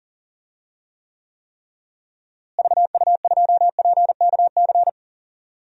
Morse code, heard as a keyed tone, about 38 wpm, 700 Hz. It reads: VU2PKC
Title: DXCC-100 Call Signs - 6 Characters - Repeated 3x - 40wpm